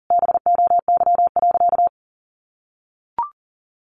Morse code, heard as a keyed tone, about 40 wpm, 700 Hz. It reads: BOY.
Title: Sentences from Top 1400 Words 40wpm